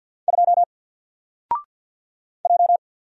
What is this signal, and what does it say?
Morse code, heard as a keyed tone, about 50 wpm, 700 Hz. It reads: 2 J
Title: Single Letter-Number 50wpm